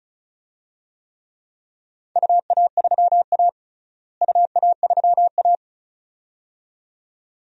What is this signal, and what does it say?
Morse code, heard as a keyed tone, about 36 wpm, 700 Hz. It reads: UA3A UA3A
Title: DXCC-100 Call Signs - 4 Characters - Repeated 2x - 35wpm